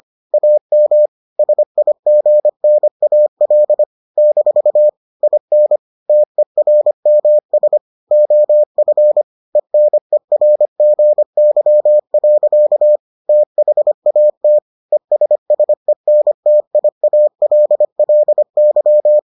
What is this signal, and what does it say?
Morse code, heard as a keyed tone, about 25 wpm, 600 Hz. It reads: AM SIGNAL - IN TERMS OF ENERGY. THAT ESSENTIALLY